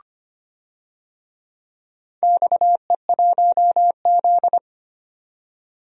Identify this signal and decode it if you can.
Morse code, heard as a keyed tone, about 25 wpm, 700 Hz. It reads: XE1Z
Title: DXCC-100 Call Signs - 4 Characters - 25wpm